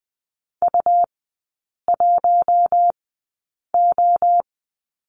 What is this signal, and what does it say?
Morse code, heard as a keyed tone, about 20 wpm, spaced as for 8 wpm, 700 Hz. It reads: U1O
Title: 100 Lesson 17e - Letter u - Three Characters 20wpm